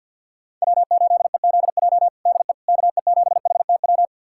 Morse code, heard as a keyed tone, about 49 wpm, 700 Hz. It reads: W8EZJ DE KE7HTW
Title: USA Callsign DE Callsign 50wpm